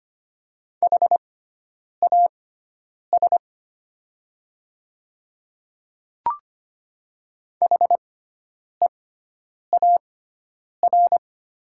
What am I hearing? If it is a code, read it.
Morse code, heard as a keyed tone, about 25 wpm, spaced as for 9 wpm, 700 Hz. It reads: HAS HEAR
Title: 6x Character Spacing - Top 300 Words - 25wpm